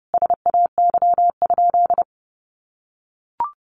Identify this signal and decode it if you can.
Morse code, heard as a keyed tone, about 30 wpm, 700 Hz. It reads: SAY?
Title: Sentences from Top 200 Words 30wpm